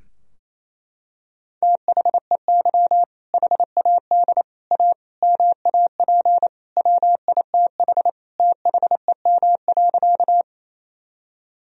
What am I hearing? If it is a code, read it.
Morse code, heard as a keyed tone, about 28 wpm, 700 Hz. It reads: THEY HAD A MAP WITH THEM.
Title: Sentences from Top 1000 Words - Review Part 2 - 28wpm